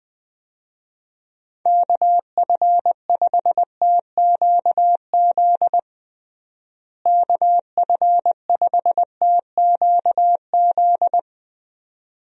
Morse code, heard as a keyed tone, about 20 wpm, 700 Hz. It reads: KF5TQZ KF5TQZ
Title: USA Call Signs - 6 Characters - Repeated 2x - 20wpm